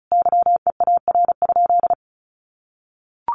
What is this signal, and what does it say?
Morse code, heard as a keyed tone, about 34 wpm, 700 Hz. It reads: YEAR?
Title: Sentences from Top 1500 Words 35wpm